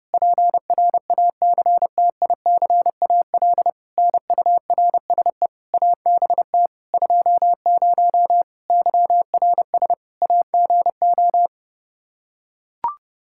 Morse code, heard as a keyed tone, about 30 wpm, 700 Hz. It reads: PRACTICAL NURSE ABT 20 YRS AGO
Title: Practice QSO - Exchange 3 - More Info - 30wpm